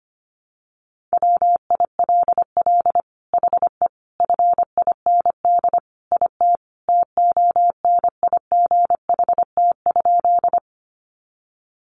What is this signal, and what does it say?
Morse code, heard as a keyed tone, about 25 wpm, 700 Hz. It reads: WILL HE FIND IT TONIGHT?
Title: Sentences from Top 600 Words 25wpm